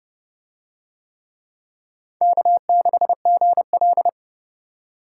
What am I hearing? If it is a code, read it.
Morse code, heard as a keyed tone, about 27 wpm, 700 Hz. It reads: K6GL